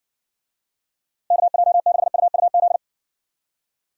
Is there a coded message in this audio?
Yes: Morse code from 1.3 to 2.8 s, heard as a keyed tone, about 59 wpm, 700 Hz. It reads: KJ6UUZ